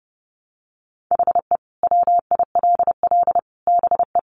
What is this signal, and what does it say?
Morse code, heard as a keyed tone, about 30 wpm, 700 Hz. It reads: HE WILL BE
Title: Sentences from Top 1100 Words 30wpm